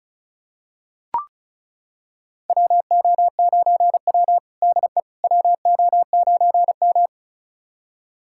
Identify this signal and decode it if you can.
Morse code, heard as a keyed tone, about 36 wpm, 700 Hz. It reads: WO9W DE WO9M
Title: USA Callsign DE Callsign - 4 Characters - 35wpm